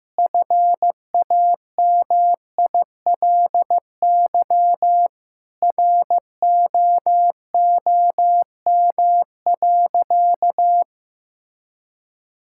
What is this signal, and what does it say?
Morse code, heard as a keyed tone, about 15 wpm, 700 Hz. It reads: FAMILY ROOM.
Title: Sentences from Top 1600 Words 15wpm